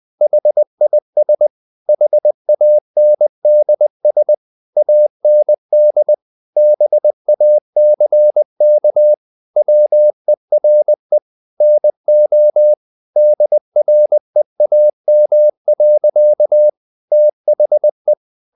Morse code, heard as a keyed tone, about 20 wpm, 600 Hz. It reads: HIS HANDS AND BACK WERE NO DREAM. THE
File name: the old man and the sea part 2 20wpm